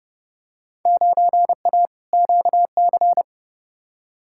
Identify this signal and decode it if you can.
Morse code, heard as a keyed tone, about 31 wpm, 700 Hz. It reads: 9A QC
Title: Field Day Exchanges Encore - 30wpm